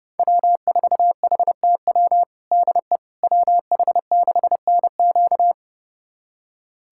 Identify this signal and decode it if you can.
Morse code, heard as a keyed tone, about 30 wpm, 700 Hz. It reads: W4HTW DE WH6NQ
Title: USA Callsign DE Callsign - 5 Characters - 30wpm